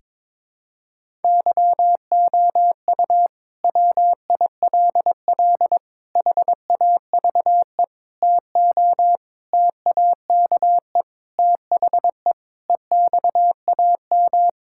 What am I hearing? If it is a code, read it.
Morse code, heard as a keyed tone, about 22 wpm, 700 Hz. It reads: YOU WILL HAVE TO TAKE THE EXAM
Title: Sentences from Top 1000 Words - Review Part 5 - 22wpm